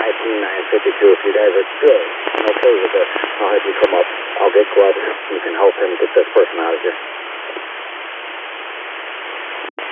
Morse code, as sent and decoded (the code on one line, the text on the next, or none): none